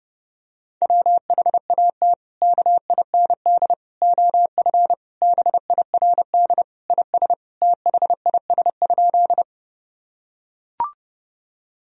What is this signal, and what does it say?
Morse code, heard as a keyed tone, about 32 wpm, 700 Hz. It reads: WHAT KIND OF BIRD IS THIS?
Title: Sentences from Top 1000 Words - Review Part 2 - 30wpm